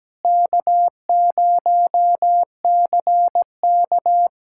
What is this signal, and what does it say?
Morse code, heard as a keyed tone, about 17 wpm, 700 Hz. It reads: K0CK
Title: USA Callsign DE Callsign - 4 Characters - 17wpm